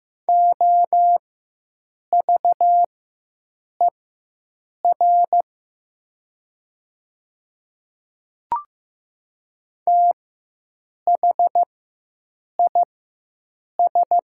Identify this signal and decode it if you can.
Morse code, heard as a keyed tone, about 15 wpm, spaced as for 7 wpm, 700 Hz. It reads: OVER THIS
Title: 4x Character Spacing - Top 100 Words - 15wpm